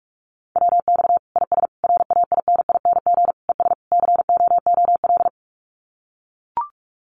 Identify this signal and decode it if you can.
Morse code, heard as a keyed tone, about 44 wpm, 700 Hz. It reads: WX IS RAINING ES COOL
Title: Common QSO Phrases 45wpm